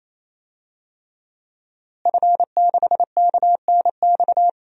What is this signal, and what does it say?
Morse code, heard as a keyed tone, about 28 wpm, 700 Hz. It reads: F6KNX
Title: DXCC-100 Call Signs - 5 Characters - Repeated 3x - 28wpm